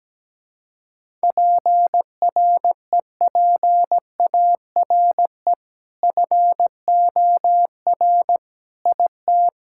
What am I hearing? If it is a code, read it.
Morse code, heard as a keyed tone, about 17 wpm, 700 Hz. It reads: PREPARE FOR IT
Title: Sentences from Top 1000 Words - Review Part 6 - 17wpm